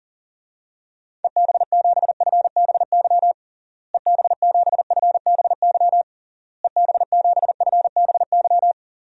Morse code, heard as a keyed tone, about 40 wpm, 700 Hz. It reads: EB7FBY EB7FBY EB7FBY
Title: DXCC-100 Call Signs - 6 Characters - Repeated 3x - 40wpm